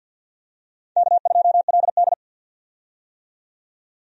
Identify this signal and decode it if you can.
Morse code, heard as a keyed tone, about 46 wpm, 700 Hz. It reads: K2LD
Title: USA Call Signs - 4 Characters - 50wpm